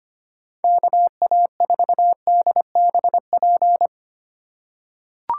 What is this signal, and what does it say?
Morse code, heard as a keyed tone, about 25 wpm, 700 Hz. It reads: KA4DBP
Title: USA Call Signs - 6 Characters - 2x Character Spacing - 25wpm